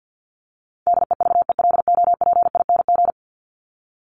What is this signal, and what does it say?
Morse code, heard as a keyed tone, about 50 wpm, 700 Hz. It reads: DEVELOPING
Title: ING Suffix - Top 100 Words with Base Words 50wpm